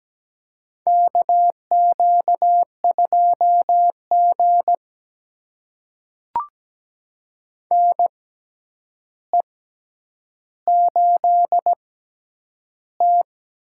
Morse code, heard as a keyed tone, about 17 wpm, spaced as for 9 wpm, 700 Hz. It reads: KQ2G NE8T
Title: USA Call Signs - 4 Characters - 6x Character Spacing - 17wpm